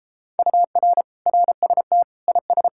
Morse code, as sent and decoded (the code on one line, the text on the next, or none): ..- .-. / .-. ... - / .. ...
UR RST IS